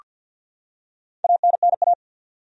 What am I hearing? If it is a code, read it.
Morse code, heard as a keyed tone, about 48 wpm, 700 Hz. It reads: ANNA
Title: USA Common Names - Max 4 Letters - 50wpm